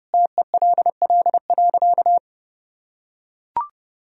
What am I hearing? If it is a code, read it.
Morse code, heard as a keyed tone, about 29 wpm, 700 Hz. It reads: TELL.
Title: Sentences from Top 300 Words and Abbreviations 30wpm